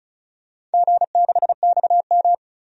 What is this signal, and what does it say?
Morse code, heard as a keyed tone, about 34 wpm, 700 Hz. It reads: G6XM
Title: DXCC-100 Call Signs - 4 Characters - 35wpm